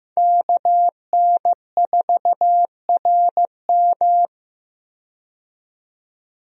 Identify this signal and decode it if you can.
Morse code, heard as a keyed tone, about 15 wpm, 700 Hz. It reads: KN4RM
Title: USA Call Signs - 5 Characters - 15wpm